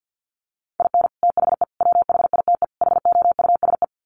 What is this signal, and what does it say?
Morse code, heard as a keyed tone, about 50 wpm, 700 Hz. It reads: IN THE WHITE HOUSE
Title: Sets of 4 Words 50wpm